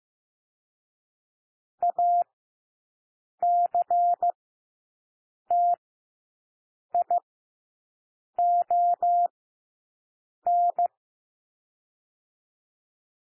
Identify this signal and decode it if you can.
Morse code, heard as a keyed tone, about 15 wpm, spaced as for 6 wpm, 700 Hz. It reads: ACTION
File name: cwmp3 / W1 SOUND WORDS 15 wpm